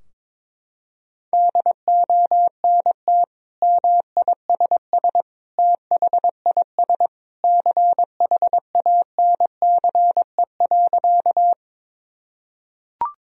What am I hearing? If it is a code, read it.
Morse code, heard as a keyed tone, about 22 wpm, 700 Hz. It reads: DONT MISS THIS CHANCE.
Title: Sentences from Top 1000 Words - Review Part 3 - 22wpm